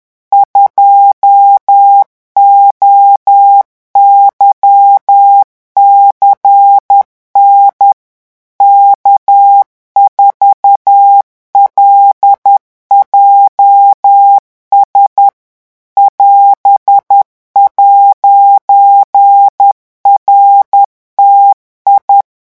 Morse code, heard as a keyed tone, about 10 wpm, 800 Hz. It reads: 2OYCN K4LJS <AS>'RTI
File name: morse mp3 / revision19-vitesse 10 mots